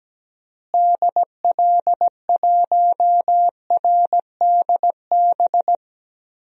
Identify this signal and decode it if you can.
Morse code, heard as a keyed tone, about 17 wpm, 700 Hz. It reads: DL1RDB
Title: DXCC-100 Call Signs - 6 Characters - Repeated 3x - 17wpm